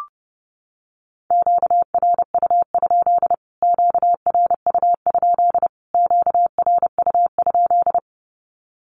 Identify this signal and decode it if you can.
Morse code, heard as a keyed tone, about 30 wpm, 700 Hz. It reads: QRU? QRU? QRU?
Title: Q-Codes Repeated 3x 30wpm